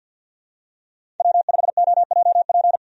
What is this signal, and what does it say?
Morse code, heard as a keyed tone, about 52 wpm, 700 Hz. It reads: W5OJP
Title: USA Call Signs - 5 Characters - 50wpm